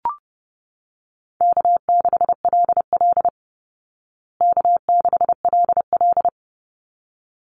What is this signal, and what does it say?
Morse code, heard as a keyed tone, about 30 wpm, 700 Hz. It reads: K6LL K6LL
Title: Easy Call Signs Repeated 2x - 30wpm